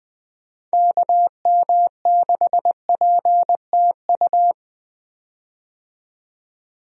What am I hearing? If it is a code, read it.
Morse code, heard as a keyed tone, about 20 wpm, 700 Hz. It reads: KM6PTU
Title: USA Call Signs - 6 Characters - 20wpm